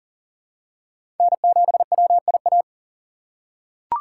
Morse code, heard as a keyed tone, about 40 wpm, 700 Hz. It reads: N7WIA
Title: USA Call Signs - 5 Characters - 2x Character Spacing - 40wpm